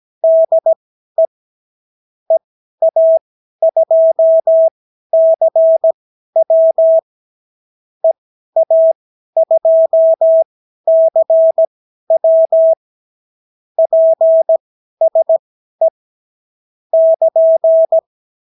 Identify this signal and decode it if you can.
Morse code, heard as a keyed tone, about 17 wpm, spaced as for 12 wpm, 650 Hz. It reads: DE EA2CW EA2CW PSE <KN>
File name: CW QSO libroCW4 18-8ppm